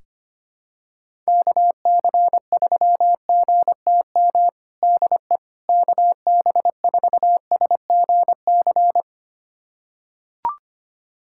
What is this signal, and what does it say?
Morse code, heard as a keyed tone, about 25 wpm, 700 Hz. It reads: KC3GTM DE KB4SGC
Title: USA Callsign DE Callsign - 6 Characters - 25wpm